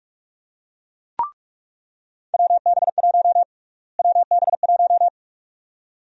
Morse code, heard as a keyed tone, about 44 wpm, 700 Hz. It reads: WB1 WB1
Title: USA Call Sign Prefixes - Repeated 2x - 45wpm